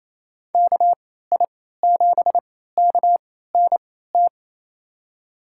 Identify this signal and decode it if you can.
Morse code, heard as a keyed tone, about 28 wpm, spaced as for 16 wpm, 700 Hz. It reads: KI7KNT
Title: USA Call Signs - 6 Characters - 3x Character Spacing - 28wpm